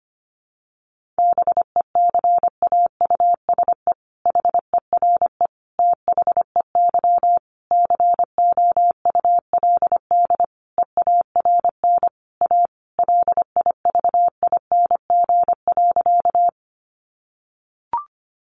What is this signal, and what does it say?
Morse code, heard as a keyed tone, about 25 wpm, 700 Hz. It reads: BECAUSE HERE THEY COULD EARN A LIVING.
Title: Sentences from Top 1500 Words 25wpm